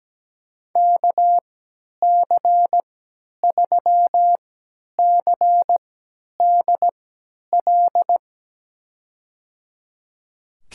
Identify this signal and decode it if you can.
Morse code, heard as a keyed tone, about 17 wpm, spaced as for 10 wpm, 700 Hz. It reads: KC3CDL
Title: USA Call Signs - 3x Character Spacing - 17wpm